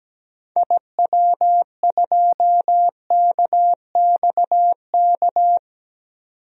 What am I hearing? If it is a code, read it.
Morse code, heard as a keyed tone, about 17 wpm, 700 Hz. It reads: IW2KXK
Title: DXCC-100 Call Signs - 6 Characters - 17wpm